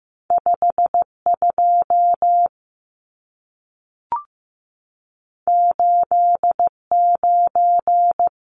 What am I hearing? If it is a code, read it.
Morse code, heard as a keyed tone, about 15 wpm, 700 Hz. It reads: 52 89
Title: Numbers - 2-Digit 15wpm